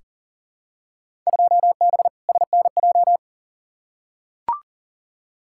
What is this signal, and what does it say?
Morse code, heard as a keyed tone, about 40 wpm, 700 Hz. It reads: 2B SNJ
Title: Field Day Exchanges Encore - 40wpm